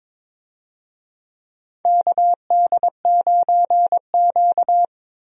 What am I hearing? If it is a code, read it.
Morse code, heard as a keyed tone, about 22 wpm, 700 Hz. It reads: KD9Q
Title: USA Call Signs - 4 Characters - Repeated 3x - 22wpm